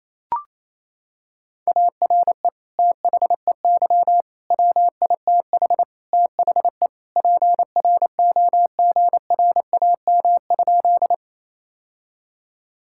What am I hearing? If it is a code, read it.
Morse code, heard as a keyed tone, about 28 wpm, 700 Hz. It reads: ARE THEY WITH THE PROGRAM?
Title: Sentences from Top 1000 Words - Review Part 5 - 28wpm